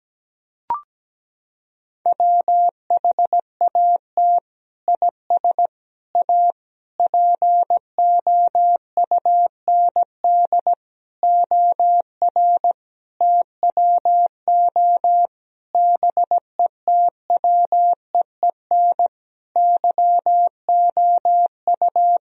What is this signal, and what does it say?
Morse code, heard as a keyed tone, about 17 wpm, 700 Hz. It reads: WHAT IS A POUND OR TWO BETWEEN YOU
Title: Sentences from Top 1000 Words - Review Part 2 - 17wpm